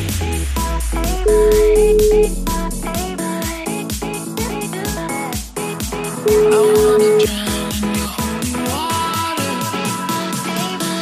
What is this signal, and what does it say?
Ringback tone (Europe, 425 Hz), with noise.